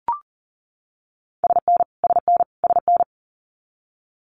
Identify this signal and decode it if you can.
Morse code, heard as a keyed tone, about 40 wpm, 700 Hz. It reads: SN SN SN
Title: Abbreviations Repeated 3x - 40wpm